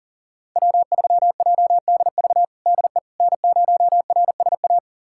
Morse code, heard as a keyed tone, about 40 wpm, 700 Hz. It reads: W3JDV DE N0RSA
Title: USA Callsign DE Callsign - 5 Characters - 40wpm